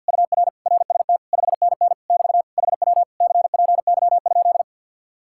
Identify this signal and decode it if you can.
Morse code, heard as a keyed tone, about 58 wpm, 700 Hz. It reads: UR RST 5NN <BT> HW CPY?
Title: Speed-Racing Common QSO Phrases - 50wpm